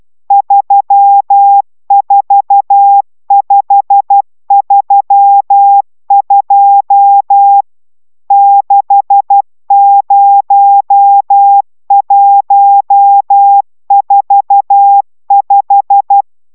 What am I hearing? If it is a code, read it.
Morse code, heard as a keyed tone, about 12 wpm, 800 Hz. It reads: 34532 60145